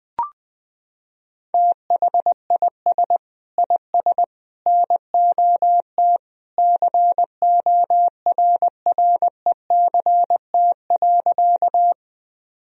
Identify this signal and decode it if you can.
Morse code, heard as a keyed tone, about 21 wpm, 700 Hz. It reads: THIS IS NOT CORRECT.
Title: Sentences from Top 1000 Words - Review Part 2 - 20wpm